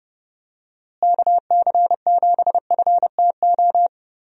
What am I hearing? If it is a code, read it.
Morse code, heard as a keyed tone, about 30 wpm, 700 Hz. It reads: KC7FTO
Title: USA Call Signs - 6 Characters - Repeated 2x - 30wpm